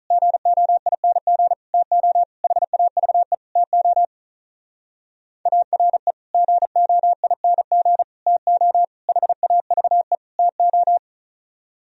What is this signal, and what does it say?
Morse code, heard as a keyed tone, about 45 wpm, 700 Hz. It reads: GOING TO HAVE TO ARE GOING TO HAVE TO